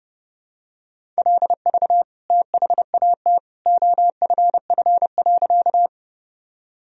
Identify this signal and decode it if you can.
Morse code, heard as a keyed tone, about 30 wpm, 700 Hz. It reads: LV THAT OFF.
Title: Sentences from Top 400 Words and Abbreviations 30wpm